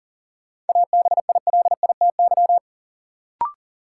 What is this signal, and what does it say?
Morse code, heard as a keyed tone, about 40 wpm, 700 Hz. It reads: ABILITY
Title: Sentences from Top 1000 Words - Review Part 3 - New Word Drill - 40wpm